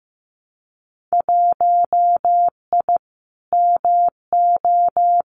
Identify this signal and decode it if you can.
Morse code, heard as a keyed tone, about 15 wpm, 700 Hz. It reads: 1I MO
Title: Winter Field Day Exchanges Encore 15wpm